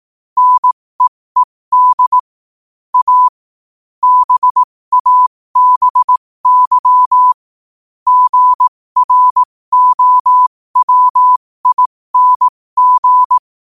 Morse code, heard as a keyed tone, about 18 wpm, spaced as for 15 wpm, 1000 Hz. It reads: NEED A BABY GROWING
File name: qotdfiles / 15wpm